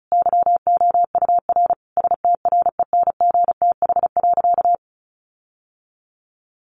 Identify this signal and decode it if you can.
Morse code, heard as a keyed tone, about 34 wpm, 700 Hz. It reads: YOUR STRENGTH.
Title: Sentences from Top 1300 Words 35wpm